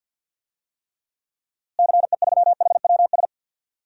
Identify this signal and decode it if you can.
Morse code, heard as a keyed tone, about 48 wpm, 700 Hz. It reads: CE3HWS
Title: DXCC-100 Call Signs - 6 Characters - Repeated 3x - 50wpm